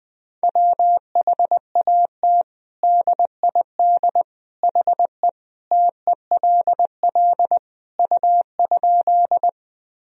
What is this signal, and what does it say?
Morse code, heard as a keyed tone, about 20 wpm, 700 Hz. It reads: WHAT DID HE TELL U?
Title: Sentences from Top 200 Words and Abbreviations 20wpm